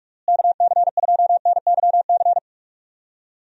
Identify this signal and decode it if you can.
Morse code, heard as a keyed tone, about 46 wpm, 700 Hz. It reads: KC2NYC
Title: USA Call Signs - 6 Characters - Repeated 3x - 45wpm